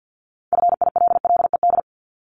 Morse code, heard as a keyed tone, about 50 wpm, 700 Hz. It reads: FILLED
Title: Multi-letter Phonemes Repeated 3x 50wpm